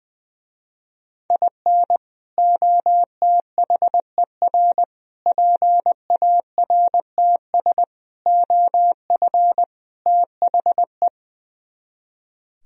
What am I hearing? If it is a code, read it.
Morse code, heard as a keyed tone, about 21 wpm, 700 Hz. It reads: IN OTHER PARTS OF THE
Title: Speed-Racing - Sets of 5 Words 20wpm